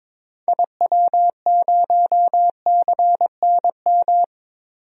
Morse code, heard as a keyed tone, about 23 wpm, 700 Hz. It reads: IW0CNM
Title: DXCC-100 Call Signs - 6 Characters - 22wpm